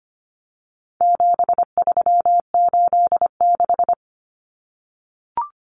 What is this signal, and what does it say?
Morse code, heard as a keyed tone, about 25 wpm, 700 Hz. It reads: 7386
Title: Common Cut Numbers - 4-Digit 25wpm